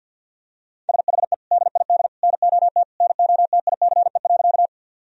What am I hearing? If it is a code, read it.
Morse code, heard as a keyed tone, about 50 wpm, 700 Hz. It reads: SHE DID NOT NOTICE.
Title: Sentences from Top 1000 Words - Review Part 2 - 50wpm